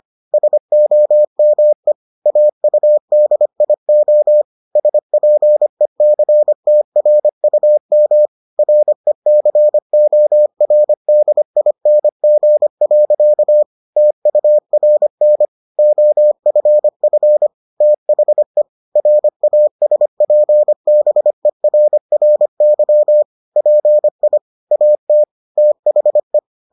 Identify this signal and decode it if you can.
Morse code, heard as a keyed tone, about 25 wpm, 600 Hz. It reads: SOME AUDIO SPECTRUM RECORDING. TURN OFF THE RASPBERRY PI AT THE